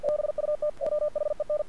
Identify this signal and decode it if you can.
Morse code, heard as a keyed tone, about 50 wpm, 600 Hz. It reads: BUT WHEN
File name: archived / CRSnet